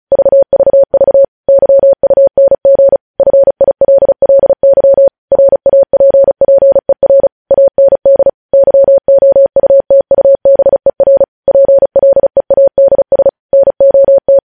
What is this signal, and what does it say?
Morse code, heard as a keyed tone, about 36 wpm, 550 Hz. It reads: VVV YUNG FILLY RAPPER AND YOUTUBER PLEADS NOT
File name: podcast / News headlines-1x-35x35